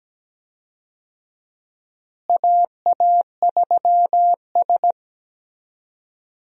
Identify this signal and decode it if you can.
Morse code, heard as a keyed tone, about 17 wpm, 700 Hz. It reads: AA3S
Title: USA Call Signs - 4 Characters - Repeated 2x - 17wpm